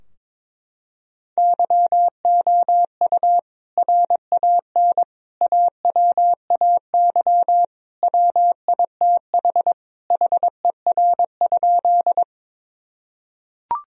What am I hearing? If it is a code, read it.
Morse code, heard as a keyed tone, about 22 wpm, 700 Hz. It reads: YOU RAN AWAY WITH HER?
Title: Sentences from Top 1000 Words - Review Part 3 - 22wpm